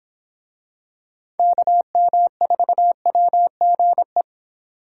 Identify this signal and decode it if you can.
Morse code, heard as a keyed tone, about 26 wpm, 700 Hz. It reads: KM4WGE